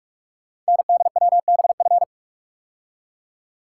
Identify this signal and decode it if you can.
Morse code, heard as a keyed tone, about 44 wpm, 700 Hz. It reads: NDWBF
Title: Code Groups - 5 Letters - 45wpm